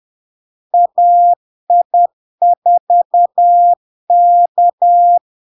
Morse code, heard as a keyed tone, about 10 wpm, 700 Hz. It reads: AI4K
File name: LARK-FD-2022-Log-in-CW-10wpm-700-Hz